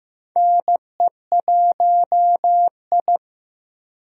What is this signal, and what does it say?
Morse code, heard as a keyed tone, about 15 wpm, 700 Hz. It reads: NE1I
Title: USA Call Signs - 4 Characters - Repeated 2x - 15wpm